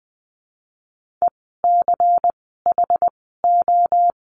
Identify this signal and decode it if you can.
Morse code, heard as a keyed tone, about 20 wpm, spaced as for 15 wpm, 700 Hz. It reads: ECHO
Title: 094 Lesson 16f - Letter c - Words 20wpm